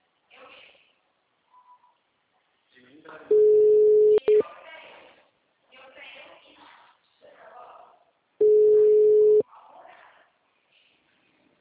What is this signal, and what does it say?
Ringback tone (Europe, 425 Hz), with speech.
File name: out- -2002- - - .66698